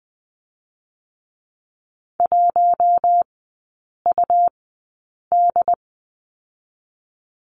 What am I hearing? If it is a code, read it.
Morse code, heard as a keyed tone, about 20 wpm, 700 Hz. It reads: 1 U D
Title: 114 Lesson 19e - Letter w - Three Characters 20wpm